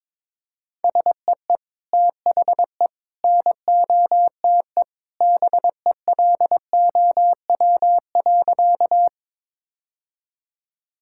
Morse code, heard as a keyed tone, about 22 wpm, 700 Hz. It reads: SEE THE NOTE BELOW.
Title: Sentences from Top 1000 Words - Review Part 5 - 22wpm